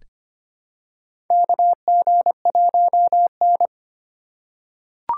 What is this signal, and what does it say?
Morse code, heard as a keyed tone, about 25 wpm, 700 Hz. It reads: KG1N
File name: practice / 25wpm / Four Letter-Number Combinations - 25wpm